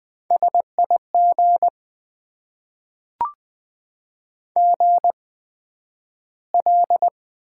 Morse code, heard as a keyed tone, about 20 wpm, spaced as for 11 wpm, 700 Hz. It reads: SIG GL
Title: Top 90 QSO Elements - 8x Character Spacing - 20wpm